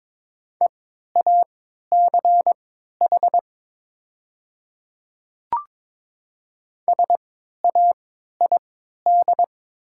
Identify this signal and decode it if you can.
Morse code, heard as a keyed tone, about 22 wpm, spaced as for 12 wpm, 700 Hz. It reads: EACH SAID
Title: 3x Character Spacing - Top 100 Words - 22wpm